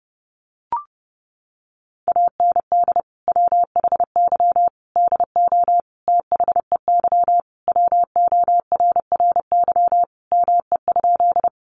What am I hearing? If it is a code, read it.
Morse code, heard as a keyed tone, about 30 wpm, 700 Hz. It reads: AND WHY DO THEY WORRY ME?
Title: Sentences from Top 1100 Words 30wpm